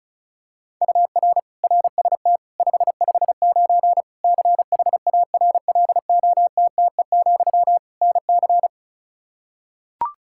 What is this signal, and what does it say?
Morse code, heard as a keyed tone, about 36 wpm, 700 Hz. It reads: UR RST 559 CHARLOTTE, NC
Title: Practice QSO - Exchange 1 - Beginning - 35wpm